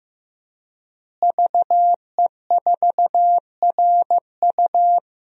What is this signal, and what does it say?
Morse code, heard as a keyed tone, about 15 wpm, 700 Hz. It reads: VE4RU
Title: DXCC-100 Call Signs - 5 Characters - Repeated 2x - 15wpm